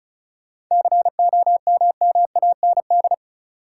Morse code, heard as a keyed tone, about 36 wpm, 700 Hz. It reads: COMMAND
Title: Sentences from Top 1600 Words - New Word Drill - 35wpm